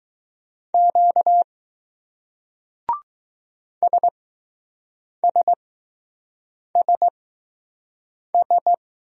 Morse code, heard as a keyed tone, about 20 wpm, spaced as for 6 wpm, 700 Hz. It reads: Q SSSS